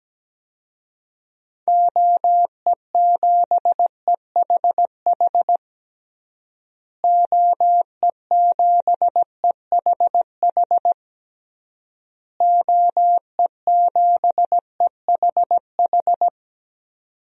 Morse code, heard as a keyed tone, about 17 wpm, 700 Hz. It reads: OE7EHH OE7EHH OE7EHH
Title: DXCC-100 Call Signs - 6 Characters - Repeated 3x - 17wpm